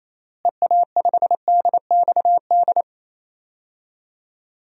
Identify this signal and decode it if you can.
Morse code, heard as a keyed tone, about 27 wpm, 700 Hz. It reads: EA5DXD
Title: DXCC-100 Call Signs - 6 Characters - Repeated 3x - 28wpm